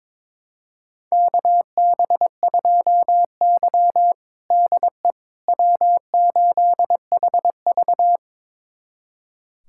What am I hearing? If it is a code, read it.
Morse code, heard as a keyed tone, about 21 wpm, 700 Hz. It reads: KB2Y DE W8HV
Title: USA Callsign DE Callsign - 4 Characters - 22wpm